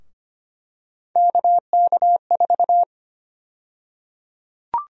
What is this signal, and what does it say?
Morse code, heard as a keyed tone, about 25 wpm, 700 Hz. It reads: KK4
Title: USA Call Sign Prefixes - Repeated 2x - 25wpm